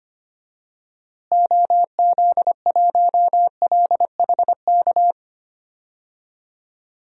Morse code, heard as a keyed tone, about 25 wpm, 700 Hz. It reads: OZ1LHK
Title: DXCC-100 Call Signs - 6 Characters - Repeated 2x - 25wpm